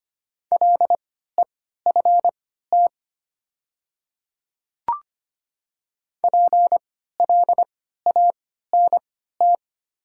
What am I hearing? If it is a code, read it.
Morse code, heard as a keyed tone, about 25 wpm, spaced as for 14 wpm, 700 Hz. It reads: LEFT PLANT
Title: 3x Character Spacing - Top 300 Words - 25wpm